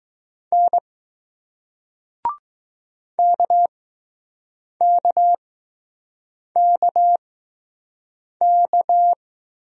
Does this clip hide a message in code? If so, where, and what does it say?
Yes: Morse code from 0.5 to 9.1 s, heard as a keyed tone, about 20 wpm, spaced as for 6 wpm, 700 Hz. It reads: N KKKK